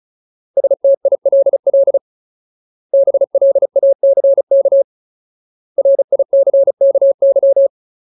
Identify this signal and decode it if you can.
Morse code, heard as a keyed tone, about 34 wpm, 550 Hz. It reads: STILL BLACK RICKY
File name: Ultimate Word List / 35WPM Ultimate